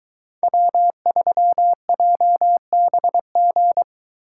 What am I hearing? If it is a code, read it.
Morse code, heard as a keyed tone, about 22 wpm, 700 Hz. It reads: W3JBG